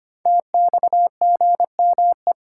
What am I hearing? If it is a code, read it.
Morse code, heard as a keyed tone, about 25 wpm, 700 Hz. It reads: TXGME
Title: Code Groups - 5 Letter:Numbers - 25wpm